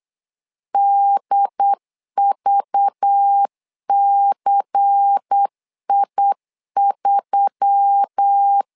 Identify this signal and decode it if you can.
Morse code, heard as a keyed tone, about 8 wpm, 800 Hz. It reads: DVCI3
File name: LEZIONI / 5 LBVC / 5 LBVC 8wpm 2935